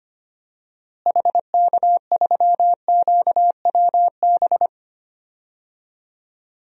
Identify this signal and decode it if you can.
Morse code, heard as a keyed tone, about 25 wpm, 700 Hz. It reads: HK3QWB
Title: DXCC-100 Call Signs - 6 Characters - Repeated 2x - 25wpm